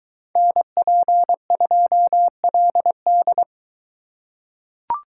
Morse code, heard as a keyed tone, about 22 wpm, 700 Hz. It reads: NP2LD